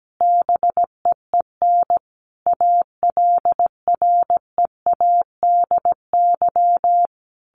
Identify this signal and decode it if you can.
Morse code, heard as a keyed tone, about 17 wpm, 700 Hz. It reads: BEEN ALREADY
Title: Sentences from Top 1500 Words 17wpm